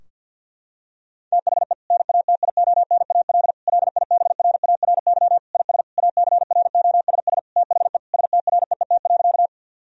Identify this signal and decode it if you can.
Morse code, heard as a keyed tone, about 50 wpm, 700 Hz. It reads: THE NATIONAL LIBRARY IS ACROSS THE STREET.
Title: Sentences from Top 1000 Words - Review Part 4 - 50wpm